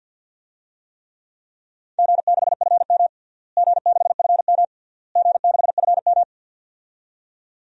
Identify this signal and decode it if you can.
Morse code, heard as a keyed tone, about 52 wpm, 700 Hz. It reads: G6FM G6FM G6FM
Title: DXCC-100 Call Signs - 4 Characters - Repeated 3x - 50wpm